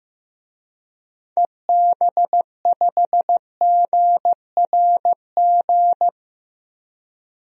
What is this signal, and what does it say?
Morse code, heard as a keyed tone, about 15 wpm, 700 Hz. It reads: EB5GRG
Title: DXCC-100 Call Signs - 6 Characters - 15wpm